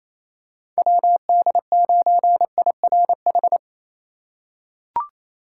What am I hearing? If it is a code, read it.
Morse code, heard as a keyed tone, about 28 wpm, 700 Hz. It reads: WD9IRH
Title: USA Call Signs - 3x Character Spacing - 28wpm